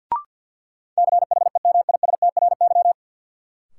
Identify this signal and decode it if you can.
Morse code, heard as a keyed tone, about 53 wpm, 700 Hz. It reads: CHEMISTRY
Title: Sentences from Top 1000 Words - Review Part 6 - New Word Drill - 50wpm